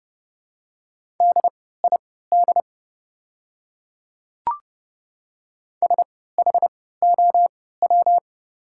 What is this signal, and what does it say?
Morse code, heard as a keyed tone, about 29 wpm, spaced as for 17 wpm, 700 Hz. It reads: DID SHOW
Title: 3x Character Spacing - Top 200 Words - 30wpm